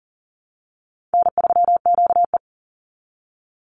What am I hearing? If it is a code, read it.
Morse code, heard as a keyed tone, about 40 wpm, 700 Hz. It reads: N3QE
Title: Easy Call Signs Repeated 2x - 40wpm